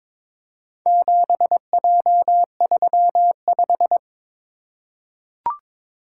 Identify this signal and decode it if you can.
Morse code, heard as a keyed tone, about 22 wpm, 700 Hz. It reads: 7J35
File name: practice / 22wpm / Four Letter-Number Combinations - 22wpm